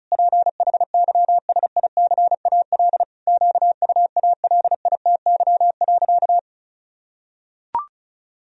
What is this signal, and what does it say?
Morse code, heard as a keyed tone, about 36 wpm, 700 Hz. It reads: PHYSICAL QUALITY.
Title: Sentences from Top 1000 Words - Review Part 4 - 35wpm